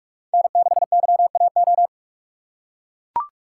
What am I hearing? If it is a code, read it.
Morse code, heard as a keyed tone, about 44 wpm, 700 Hz. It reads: N6YAO
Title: USA Call Signs - 5 Characters - 2x Character Spacing - 45wpm